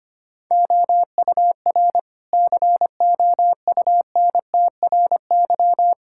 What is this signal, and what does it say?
Morse code, heard as a keyed tone, about 25 wpm, 700 Hz. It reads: OUR COUNTRY
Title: Sentences from Top 1000 Words - Review Part 6 - 25wpm